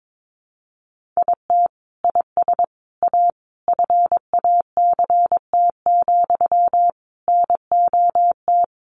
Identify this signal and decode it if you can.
Morse code, heard as a keyed tone, about 22 wpm, 700 Hz. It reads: IT IS A FACT, NOT
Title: Sentences from Top 1400 Words 22wpm